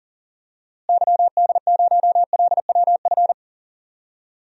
Morse code, heard as a keyed tone, about 40 wpm, 700 Hz. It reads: YD0LWF
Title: DXCC-100 Call Signs - 6 Characters - Repeated 3x - 40wpm